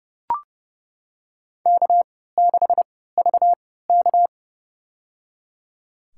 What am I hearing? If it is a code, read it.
Morse code, heard as a keyed tone, about 29 wpm, spaced as for 17 wpm, 700 Hz. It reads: K6VK
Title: USA Call Signs - 4 Characters - 3x Character Spacing - 30wpm